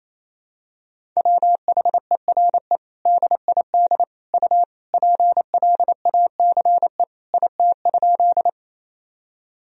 Morse code, heard as a keyed tone, about 28 wpm, 700 Hz. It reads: WHERE DID U PLACE IT?
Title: Sentences from Top 200 Words and Abbreviations 28wpm